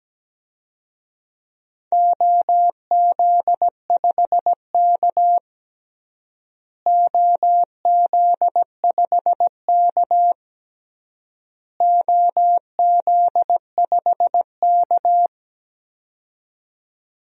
Morse code, heard as a keyed tone, about 17 wpm, 700 Hz. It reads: OZ5K OZ5K OZ5K
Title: DXCC-100 Call Signs - 4 Characters - Repeated 3x - 17wpm